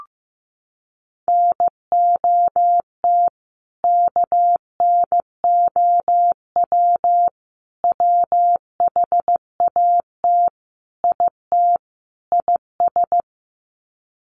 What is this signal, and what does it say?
Morse code, heard as a keyed tone, about 15 wpm, 700 Hz. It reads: NOT KNOW WHAT IT IS
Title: Sets of 5 Words Encore 15wpm